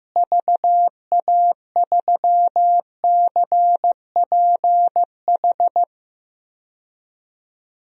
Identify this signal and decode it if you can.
Morse code, heard as a keyed tone, about 15 wpm, 700 Hz. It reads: VA3CPH
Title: DXCC-100 Call Signs - 6 Characters - Repeated 3x - 15wpm